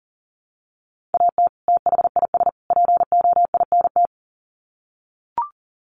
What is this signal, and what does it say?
Morse code, heard as a keyed tone, about 40 wpm, 700 Hz. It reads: AT THIS POINT
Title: Sets of 3 Words 40wpm